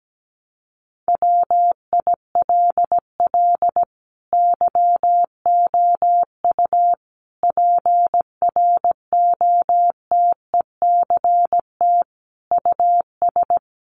Morse code, heard as a keyed tone, about 17 wpm, 700 Hz. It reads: WILL YOU PROTECT US
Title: Sentences from Top 900 Words 17wpm